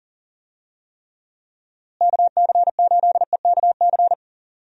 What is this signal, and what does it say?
Morse code, heard as a keyed tone, about 40 wpm, 700 Hz. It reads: KC8EKC
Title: USA Call Signs - 6 Characters - Repeated 2x - 40wpm